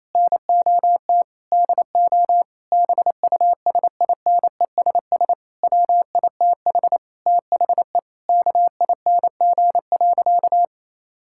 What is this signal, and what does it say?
Morse code, heard as a keyed tone, about 28 wpm, 700 Hz. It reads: NOT DO BUSINESS WITH THE KING.
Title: Sentences from Top 1000 Words - Review Part 6 - 28wpm